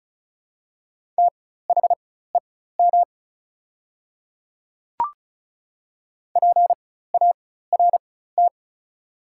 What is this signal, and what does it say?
Morse code, heard as a keyed tone, about 36 wpm, spaced as for 16 wpm, 700 Hz. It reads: THEM PART
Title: 4x Character Spacing - Top 200 Words - 35wpm